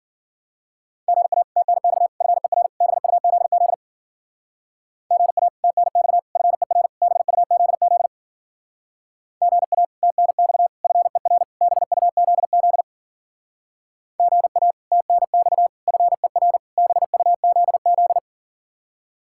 Morse code, heard as a keyed tone, about 52 wpm, 700 Hz. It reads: GA TNX FER BUZZ GA TNX FER BUZZ GA TNX FER BUZZ GA TNX FER BUZZ